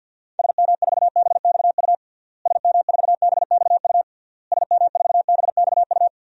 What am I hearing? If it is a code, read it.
Morse code, heard as a keyed tone, about 52 wpm, 700 Hz. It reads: SM4BXU SM4BXU SM4BXU
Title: DXCC-100 Call Signs - 6 Characters - Repeated 3x - 50wpm